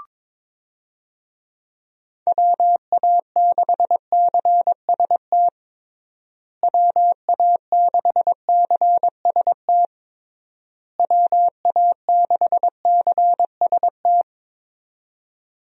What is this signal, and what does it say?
Morse code, heard as a keyed tone, about 22 wpm, 700 Hz. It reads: WA6CST WA6CST WA6CST
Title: USA Call Signs - 6 Characters - Repeated 3x - 22wpm